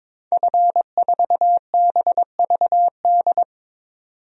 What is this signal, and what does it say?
Morse code, heard as a keyed tone, about 22 wpm, 700 Hz. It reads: F4BVD
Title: DXCC-100 Call Signs - 5 Characters - 22wpm